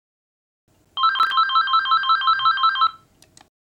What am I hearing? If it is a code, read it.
There is no code.